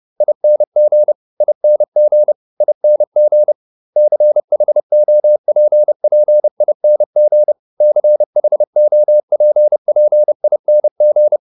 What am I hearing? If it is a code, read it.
Morse code, heard as a keyed tone, about 30 wpm, 600 Hz. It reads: ING ING ING CHOPPING CHOPPING
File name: ING8 30